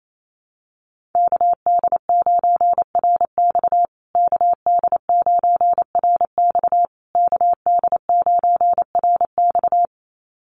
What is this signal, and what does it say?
Morse code, heard as a keyed tone, about 28 wpm, 700 Hz. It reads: KD9RX KD9RX KD9RX
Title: USA Call Signs Repeated 3x 28wpm